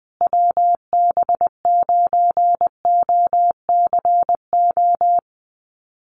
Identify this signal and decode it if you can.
Morse code, heard as a keyed tone, about 20 wpm, 700 Hz. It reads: WB9OCO
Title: USA Call Signs Repeated 3x 20wpm